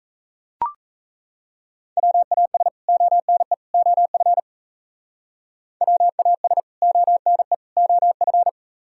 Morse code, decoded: WAS ONE OF WAS ONE OF

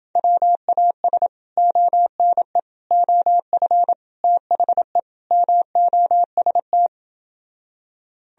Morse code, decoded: WAS ONE OF THE MOST

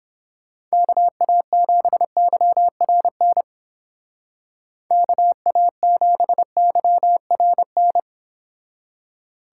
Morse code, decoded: KA7YRN KA7YRN

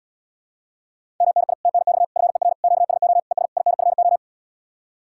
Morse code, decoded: MNI TNX FER QSO ES INFO

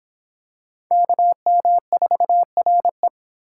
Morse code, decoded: KM4RE

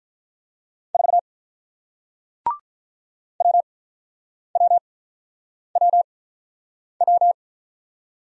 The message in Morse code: 4 WWWW

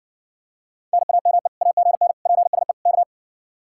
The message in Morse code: NAME AGN PSE K

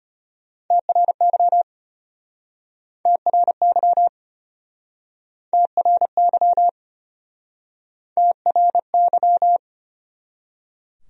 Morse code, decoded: TRY TRY TRY TRY

